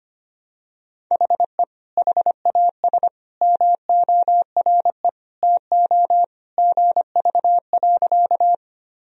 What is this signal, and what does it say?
Morse code, heard as a keyed tone, about 25 wpm, 700 Hz. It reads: HE HAS MORE TO GV.
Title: Sentences from Top 200 Words and Abbreviations 25wpm